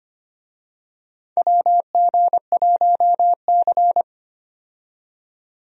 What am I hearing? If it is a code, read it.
Morse code, heard as a keyed tone, about 25 wpm, 700 Hz. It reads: WG1C
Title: USA Call Signs - 4 Characters - 25wpm